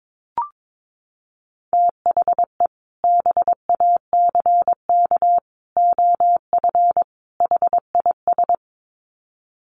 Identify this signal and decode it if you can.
Morse code, heard as a keyed tone, about 22 wpm, 700 Hz. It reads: THE BACK OF HIS
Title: Sets of 4 Words 22wpm